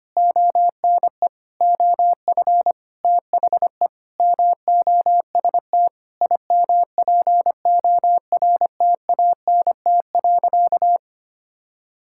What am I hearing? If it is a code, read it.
Morse code, heard as a keyed tone, about 25 wpm, 700 Hz. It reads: ONE OF THE MOST IMPORTANT.
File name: practice / 25wpm / Sentences with 9-Word Phrases - 25wpm